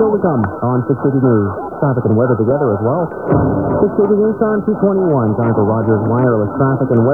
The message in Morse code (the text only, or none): none